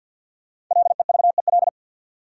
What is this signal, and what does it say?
Morse code, heard as a keyed tone, about 51 wpm, 700 Hz. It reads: LEVEL
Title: Sentences from Top 700 Words - New Word Drill - 50wpm